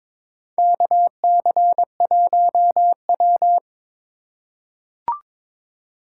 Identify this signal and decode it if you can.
Morse code, heard as a keyed tone, about 22 wpm, 700 Hz. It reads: KC1W
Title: USA Call Signs - 2x Character Spacing - 22wpm